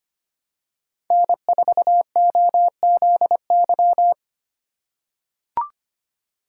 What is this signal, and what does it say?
Morse code, heard as a keyed tone, about 25 wpm, 700 Hz. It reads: N4OZY
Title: USA Call Signs - 5 Characters - 4x Character Spacing - 25wpm